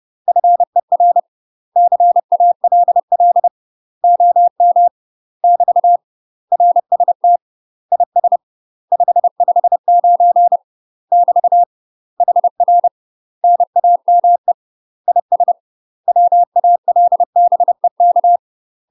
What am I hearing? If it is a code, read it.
Morse code, heard as a keyed tone, about 30 wpm, 700 Hz. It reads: FER CALL OM <BT> RST IS 559 <BT> HR NAME IS WALBEK
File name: UK MASTER QSOs TRAINING / 30WPM / UK MASTER QSOs 0027